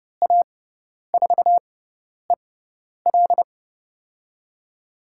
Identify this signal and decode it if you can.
Morse code, heard as a keyed tone, about 30 wpm, spaced as for 10 wpm, 700 Hz. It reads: A4EL
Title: USA Call Signs - 5 Characters - 6x Character Spacing - 30wpm